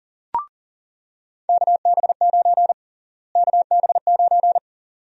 Morse code, heard as a keyed tone, about 40 wpm, 700 Hz. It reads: KB9 KB9
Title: USA Call Sign Prefixes - Repeated 2x - 40wpm